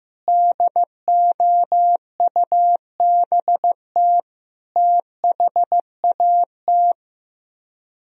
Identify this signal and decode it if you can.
Morse code, heard as a keyed tone, about 15 wpm, 700 Hz. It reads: DOUBT THAT
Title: Speed-Racing - Sets of 5 Words 15wpm